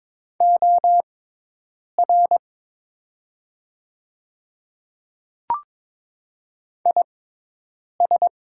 Morse code, heard as a keyed tone, about 22 wpm, spaced as for 8 wpm, 700 Hz. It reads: OR IS
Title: 6x Character Spacing - Top 100 Words - 22wpm